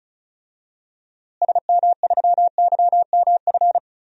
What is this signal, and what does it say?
Morse code, heard as a keyed tone, about 36 wpm, 700 Hz. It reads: SM3YMF
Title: DXCC-100 Call Signs - 6 Characters - Repeated 2x - 35wpm